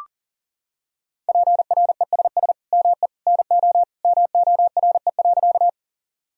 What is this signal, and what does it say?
Morse code, heard as a keyed tone, about 40 wpm, 700 Hz. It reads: PRESS ME NO MORE.
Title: Sentences from Top 1000 Words - Review Part 1 - 40wpm